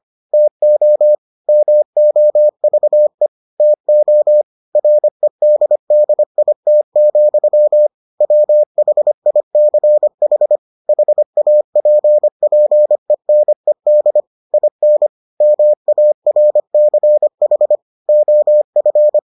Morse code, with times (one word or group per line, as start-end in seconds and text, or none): 0.3-1.1 TO
1.5-3.3 MOVE
3.6-4.4 TO
4.8-7.9 REDDIT,
8.2-10.6 WHICH
10.9-14.2 HAPPENED
14.5-15.1 IN
15.4-17.8 MARCH
18.1-19.2 OF